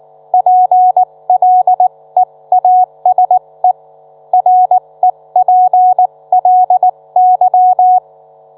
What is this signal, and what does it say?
Morse code, heard as a keyed tone, about 19 wpm, spaced as for 17 wpm, 730 Hz. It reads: PLEASE REPLY